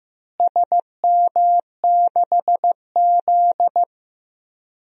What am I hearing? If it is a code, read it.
Morse code, heard as a keyed tone, about 15 wpm, 700 Hz. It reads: SM6Z
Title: DXCC-100 Call Signs - 4 Characters - Repeated 2x - 15wpm